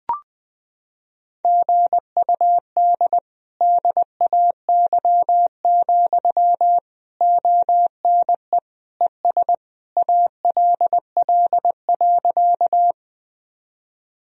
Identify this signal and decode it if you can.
Morse code, heard as a keyed tone, about 21 wpm, 700 Hz. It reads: GUD DAY, ONE ES ALL.
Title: Sentences from Top 200 Words and Abbreviations 20wpm